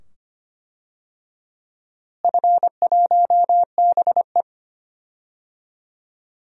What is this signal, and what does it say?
Morse code, heard as a keyed tone, about 25 wpm, 700 Hz. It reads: F1BE
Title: DXCC-100 Call Signs - 4 Characters - Repeated 2x - 25wpm